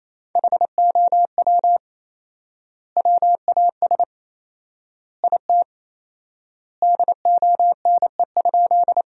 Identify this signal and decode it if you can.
Morse code, heard as a keyed tone, about 28 wpm, 700 Hz. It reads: HOW WAS IT DONE?
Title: 4x Word-Spacing - Sentences from Top 500 Words - 28wpm